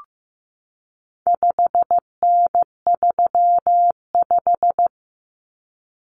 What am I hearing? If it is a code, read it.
Morse code, heard as a keyed tone, about 15 wpm, 700 Hz. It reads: 5N35
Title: Common Cut Numbers - 4-Digit 15wpm